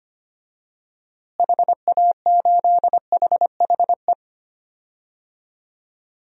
Morse code, heard as a keyed tone, about 25 wpm, 700 Hz. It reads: HA8HHE
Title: DXCC-100 Call Signs - 6 Characters - Repeated 2x - 25wpm